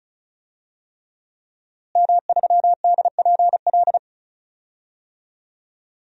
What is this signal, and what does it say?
Morse code, heard as a keyed tone, about 34 wpm, 700 Hz. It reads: M3DPL
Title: DXCC-100 Call Signs - 5 Characters - Repeated 3x - 35wpm